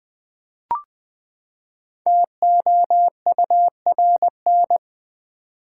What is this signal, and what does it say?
Morse code, heard as a keyed tone, about 20 wpm, 700 Hz. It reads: TOURN
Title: Build A Word - 5 Letters At A Time - 10-Letter Words - 20wpm